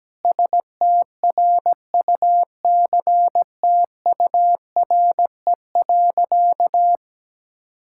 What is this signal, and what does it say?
Morse code, heard as a keyed tone, about 17 wpm, 700 Hz. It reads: STRUCTURE.
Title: Sentences from Top 1000 Words - Review Part 6 - 17wpm